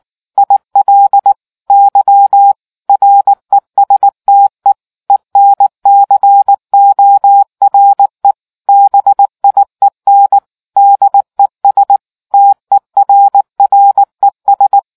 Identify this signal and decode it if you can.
Morse code, heard as a keyed tone, about 19 wpm, 800 Hz. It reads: IL Y RESTE ENCORE BIEN DES TERRES